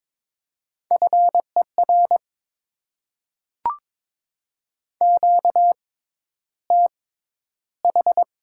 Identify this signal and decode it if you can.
Morse code, heard as a keyed tone, about 22 wpm, spaced as for 11 wpm, 700 Hz. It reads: FER QTH
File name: Top 20 QSO Elements - 6x Character Spacing - 22wpm